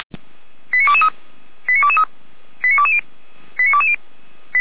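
Five-tone selective call (ZVEI), heard as quick runs of five tones, tones of about 70 ms. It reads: 89223 89223 89220 89220
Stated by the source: ZVEI five-tone call 89119 received by SDR.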